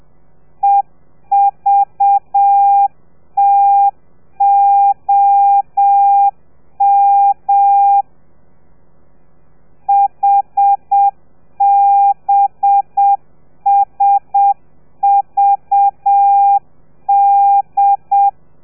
Morse code, heard as a keyed tone, about 7 wpm, 800 Hz. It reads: EVTOM HBSVD
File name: telegrafia / i2tfj / Lez4 30